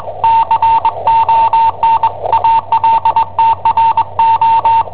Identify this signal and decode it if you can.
Morse code, heard as a keyed tone, about 21 wpm, 910 Hz, with radio noise. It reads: CON ALTRO
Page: Cassetta esercizi preparata da I7OHP per IW5ELP